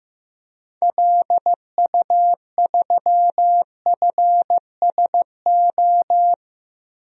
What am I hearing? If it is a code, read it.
Morse code, heard as a keyed tone, about 15 wpm, 700 Hz. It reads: LU3FSO
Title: DXCC-100 Call Signs - 6 Characters - Repeated 2x - 15wpm